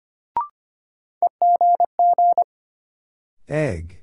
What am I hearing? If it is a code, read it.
Morse code, heard as a keyed tone, about 25 wpm, 700 Hz. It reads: EGG
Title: Sentences from Top 1000 Words - Review Part 7 - New Word Drill - 25wpm